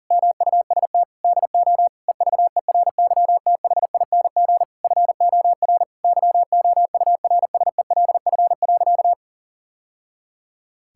Morse code, heard as a keyed tone, about 40 wpm, 700 Hz. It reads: MUST DO EVERYTHING FOR YOURSELF.
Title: Sentences from Top 1000 Words - Review Part 6 - 40wpm